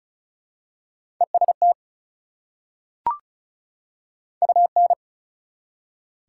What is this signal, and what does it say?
Morse code, heard as a keyed tone, about 36 wpm, 700 Hz. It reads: EST UN
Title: Prefixes and Suffixes 35wpm